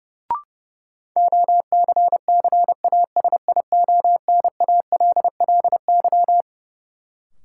Morse code, heard as a keyed tone, about 30 wpm, 700 Hz. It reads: OCCASIONALLY
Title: Sentences from Top 1300 Words - New Word Drill Alt - 30wpm